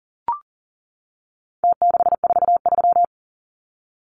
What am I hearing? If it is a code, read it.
Morse code, heard as a keyed tone, about 40 wpm, 700 Hz. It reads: T643
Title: Numbers - Contest Style - 4-Digit 40wpm